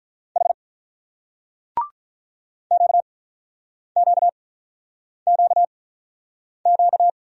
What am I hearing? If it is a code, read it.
Morse code, heard as a keyed tone, about 43 wpm, spaced as for 8 wpm, 690 Hz. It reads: H QQQQ
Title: Speed-Racing Single Letters 35wpm